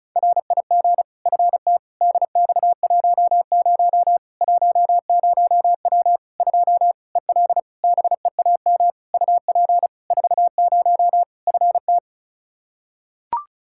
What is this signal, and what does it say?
Morse code, heard as a keyed tone, about 36 wpm, 700 Hz. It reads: RIG FT DX10 10W 2 EL BEAM UP 40 FT
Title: Practice QSO - Exchange 2 - Info - 35wpm